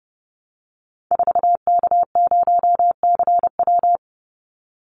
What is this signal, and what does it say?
Morse code, heard as a keyed tone, about 30 wpm, 700 Hz. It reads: 4K0CW
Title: Call Signs Repeated 3x - 30wpm